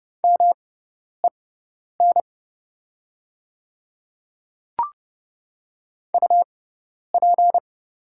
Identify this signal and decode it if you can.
Morse code, heard as a keyed tone, about 29 wpm, spaced as for 10 wpm, 700 Hz. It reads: MEN UP
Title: 6x Character Spacing - Top 300 Words - 30wpm